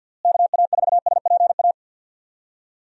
Morse code, heard as a keyed tone, about 51 wpm, 700 Hz. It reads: KA4SPA
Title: USA Call Signs - 6 Characters - Repeated 2x - 50wpm